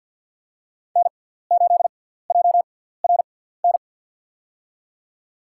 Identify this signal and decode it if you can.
Morse code, heard as a keyed tone, about 54 wpm, spaced as for 17 wpm, 700 Hz. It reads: N8JRN
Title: USA Call Signs - 5 Characters - 6x Character Spacing - 50wpm